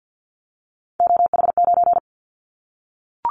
Morse code, heard as a keyed tone, about 50 wpm, 700 Hz. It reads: OH9
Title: Callsign Prefixes for Top 50 DXCC - 50wpm